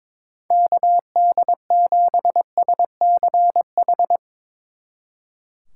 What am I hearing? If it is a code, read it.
Morse code, heard as a keyed tone, about 24 wpm, 700 Hz. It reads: KD7SCH
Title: Speed Racing - USA 2x3 Call Signs 22wpm